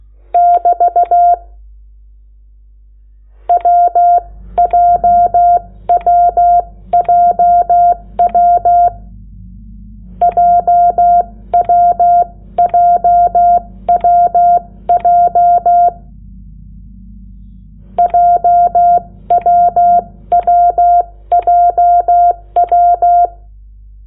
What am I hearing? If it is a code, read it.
Morse code, heard as a keyed tone, about 16 wpm, spaced as for 14 wpm, 700 Hz. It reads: <BT> WJWJW JWJWJ JWWJW
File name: Lecciones / CLASE Nº8